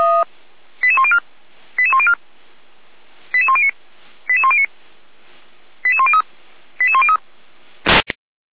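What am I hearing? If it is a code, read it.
Five-tone selective call (ZVEI), heard as quick runs of five tones, tones of about 70 ms. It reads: 88104 88104 88109 88109 88103 88103
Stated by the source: Five-tone ZVEI selective call received by SDR, from the feuerwehr-orlamuende.org fire brigade audio server.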